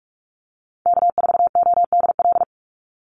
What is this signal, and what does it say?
Morse code, heard as a keyed tone, about 44 wpm, 700 Hz. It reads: K4ODL
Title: Call Signs - 45wpm